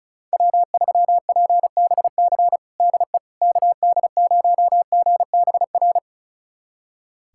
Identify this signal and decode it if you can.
Morse code, heard as a keyed tone, about 34 wpm, 700 Hz. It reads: W3PBC DE KD0GBR
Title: USA Callsign DE Callsign 35wpm